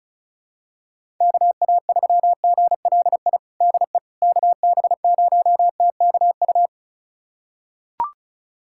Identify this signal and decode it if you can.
Morse code, heard as a keyed tone, about 34 wpm, 700 Hz. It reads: KA3GLI DE KB0TKU
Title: USA Callsign DE Callsign - 6 Characters - 35wpm